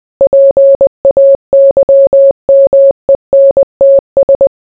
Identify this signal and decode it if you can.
Morse code, heard as a keyed tone, about 20 wpm, 550 Hz. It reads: PAYMENTS